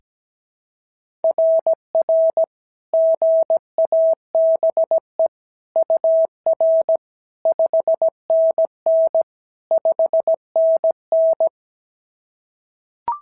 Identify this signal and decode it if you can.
Morse code, heard as a keyed tone, about 16 wpm, 650 Hz. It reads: RR GABE UR 5NN 5NN
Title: SOTA QSO Practice 17wpm